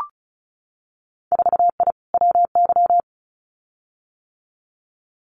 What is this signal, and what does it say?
Morse code, heard as a keyed tone, about 36 wpm, 700 Hz. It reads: 4I WY
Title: Winter Field Day Exchanges Encore 35wpm